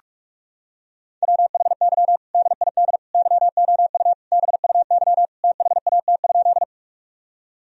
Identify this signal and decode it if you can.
Morse code, heard as a keyed tone, about 41 wpm, 700 Hz. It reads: WHY DID YOU BUY THAT?
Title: Sentences from Top 1000 Words - Review Part 4 - 45wpm